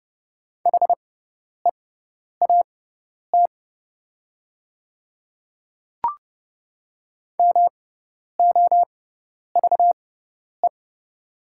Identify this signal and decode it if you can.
Morse code, heard as a keyed tone, about 29 wpm, spaced as for 10 wpm, 690 Hz. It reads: HEAT MOVE
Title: 6x Character Spacing - Top 500 Words - 30wpm